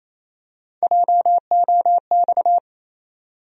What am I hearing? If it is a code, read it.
Morse code, heard as a keyed tone, about 28 wpm, 700 Hz. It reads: JOX
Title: USA Call Sign Postfixes - 3 Letters 28wpm